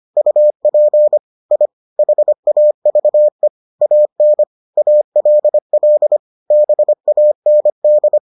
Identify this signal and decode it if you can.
Morse code, heard as a keyed tone, about 25 wpm, 600 Hz. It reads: UP I HAVE AN ALL BAND